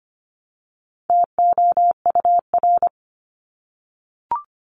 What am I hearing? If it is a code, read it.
Morse code, heard as a keyed tone, about 25 wpm, 700 Hz. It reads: TOUR
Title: Top 1251-1500 Words 25wpm